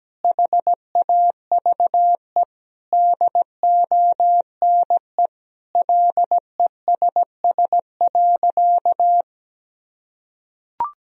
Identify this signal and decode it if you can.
Morse code, heard as a keyed tone, about 17 wpm, 700 Hz. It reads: HAVE DONE LESS.
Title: Sentences from Top 1000 Words - Review Part 3 - 17wpm